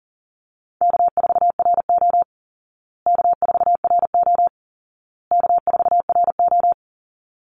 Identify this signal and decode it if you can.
Morse code, heard as a keyed tone, about 40 wpm, 700 Hz. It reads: K4RO K4RO K4RO
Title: Easy Call Signs Repeated 3x - 40wpm